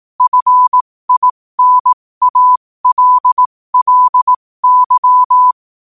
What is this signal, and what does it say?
Morse code, heard as a keyed tone, about 18 wpm, spaced as for 15 wpm, 1000 Hz. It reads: FINALLY
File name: qotdfiles / 15wpm